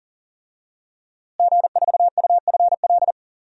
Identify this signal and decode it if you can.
Morse code, heard as a keyed tone, about 40 wpm, 700 Hz. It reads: G4UFL
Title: DXCC-100 Call Signs - 5 Characters - Repeated 3x - 40wpm